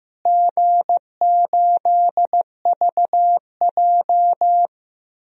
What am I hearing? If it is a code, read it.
Morse code, heard as a keyed tone, about 15 wpm, 700 Hz. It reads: G8VJ
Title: DXCC-100 Call Signs - 4 Characters - Repeated 3x - 15wpm